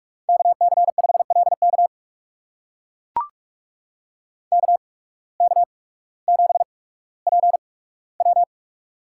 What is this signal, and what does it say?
Morse code, heard as a keyed tone, about 42 wpm, spaced as for 19 wpm, 700 Hz. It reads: KC5LK KK7PW
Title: USA Call Signs - 8x Character Spacing - 45wpm